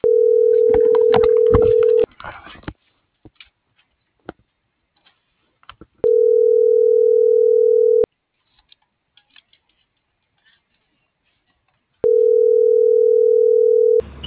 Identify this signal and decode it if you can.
Ringback tone (North America), 440 + 480 Hz.
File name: CENTRAL GOMSA / out garcia